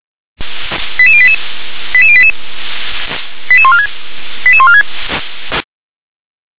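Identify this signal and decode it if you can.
Five-tone selective call (ZVEI), heard as quick runs of five tones, tones of about 70 ms. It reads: 88088 88088 88146 88146
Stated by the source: ZVEI five-tone selective call received on SDR, code 88154.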